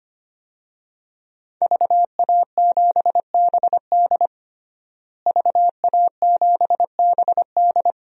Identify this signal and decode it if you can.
Morse code, heard as a keyed tone, about 25 wpm, 700 Hz. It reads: VA7BD VA7BD
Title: DXCC-100 Call Signs - 5 Characters - Repeated 2x - 25wpm